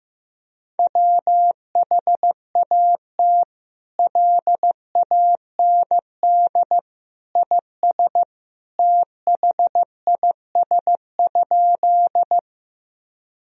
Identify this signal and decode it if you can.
Morse code, heard as a keyed tone, about 15 wpm, 700 Hz. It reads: WHAT LAND IS THIS?
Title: Sentences from Top 1000 Words - Review Part 1 - 15wpm